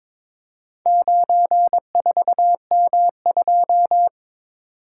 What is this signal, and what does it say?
Morse code, heard as a keyed tone, about 23 wpm, 700 Hz. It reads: 94M2
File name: practice / 22wpm / Four Letter-Number Combinations - 22wpm